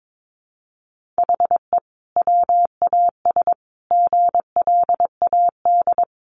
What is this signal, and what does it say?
Morse code, heard as a keyed tone, about 22 wpm, 700 Hz. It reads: HE WAS GLAD
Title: Sentences from Top 1500 Words 22wpm